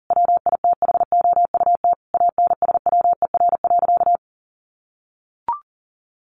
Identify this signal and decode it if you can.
Morse code, heard as a keyed tone, about 40 wpm, 700 Hz. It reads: WITHOUT ANSWER.
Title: Sentences from Top 1500 Words 40wpm